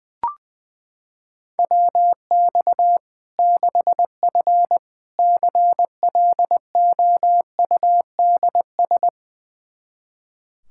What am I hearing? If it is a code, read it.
Morse code, heard as a keyed tone, about 21 wpm, 700 Hz. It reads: WX 6F CLOUDS
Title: Practice QSO - Exchange 2 - Info - 20wpm